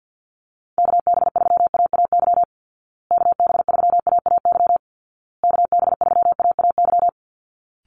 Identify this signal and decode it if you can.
Morse code, heard as a keyed tone, about 49 wpm, 700 Hz. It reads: KB3AAY KB3AAY KB3AAY
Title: Call Signs Repeated 2x - 50wpm